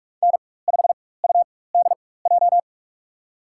Morse code, heard as a keyed tone, about 47 wpm, spaced as for 21 wpm, 700 Hz. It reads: N5UDJ
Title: USA Call Signs - 5 Characters - 4x Character Spacing - 45wpm